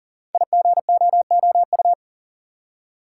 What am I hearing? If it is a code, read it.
Morse code, heard as a keyed tone, about 38 wpm, 690 Hz. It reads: IGOOU
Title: Code Groups - 5 Letters - 40wpm